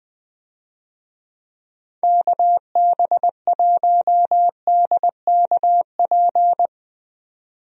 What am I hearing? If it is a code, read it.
Morse code, heard as a keyed tone, about 20 wpm, 700 Hz. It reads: KB1DKP
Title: USA Call Signs - 6 Characters - 20wpm